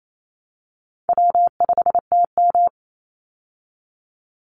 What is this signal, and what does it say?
Morse code, heard as a keyed tone, about 28 wpm, 700 Hz. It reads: W5TM
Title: Easy Call Signs Repeated 2x - 28wpm